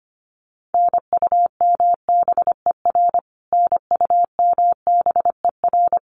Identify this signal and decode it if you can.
Morse code, heard as a keyed tone, about 25 wpm, 700 Hz. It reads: NUMBER NUMBER
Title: Top 200 Words Repeated 25wpm